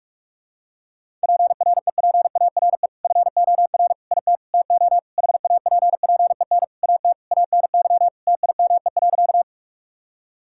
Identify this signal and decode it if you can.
Morse code, heard as a keyed tone, about 46 wpm, 700 Hz. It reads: PREPARE FOR IT TO HAPPEN AT ANY TIME.
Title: Sentences from Top 1000 Words - Review Part 6 - 45wpm